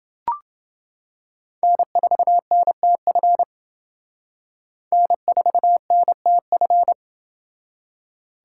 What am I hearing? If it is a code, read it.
Morse code, heard as a keyed tone, about 28 wpm, 700 Hz. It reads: N4NTF N4NTF